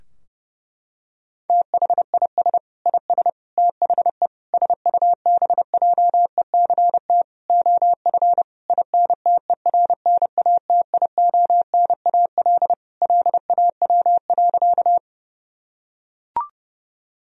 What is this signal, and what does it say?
Morse code, heard as a keyed tone, about 32 wpm, 700 Hz. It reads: THIS IS THE SUBJECT OF INTERNATIONAL LAW.
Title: Sentences from Top 1000 Words - Review Part 4 - 30wpm